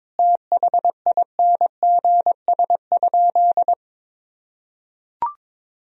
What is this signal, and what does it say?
Morse code, heard as a keyed tone, about 21 wpm, 700 Hz. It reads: THINGS?
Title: Sentences from Top 1000 Words - Review Part 4 - 22wpm